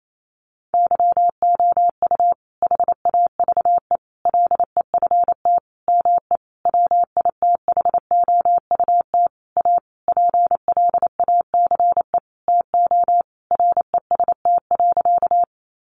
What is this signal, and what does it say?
Morse code, heard as a keyed tone, about 28 wpm, 700 Hz. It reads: YOU HAVE LEFT ME WITHOUT A PLACE TO REST.
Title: Sentences from Top 600 Words 28wpm